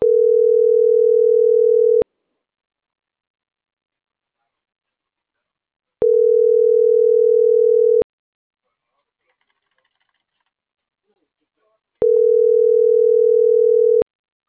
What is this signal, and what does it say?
Ringback tone (North America), 440 + 480 Hz.